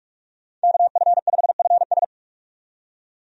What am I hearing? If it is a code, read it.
Morse code, heard as a keyed tone, about 46 wpm, 700 Hz. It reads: KF5FS
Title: USA Call Signs - 5 Characters - 45wpm